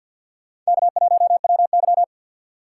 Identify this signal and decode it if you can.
Morse code, heard as a keyed tone, about 63 wpm, 700 Hz. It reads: K1WY